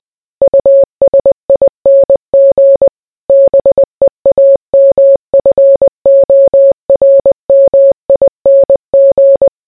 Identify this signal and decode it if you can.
Morse code, heard as a keyed tone, about 20 wpm, 550 Hz. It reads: USING BEAMFORMING